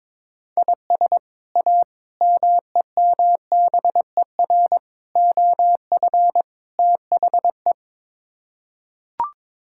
Morse code, heard as a keyed tone, about 22 wpm, 700 Hz. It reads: IS A MEMBER OF THE
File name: practice / 22wpm / Build A Phrase - 5-Word Phrases - 22wpm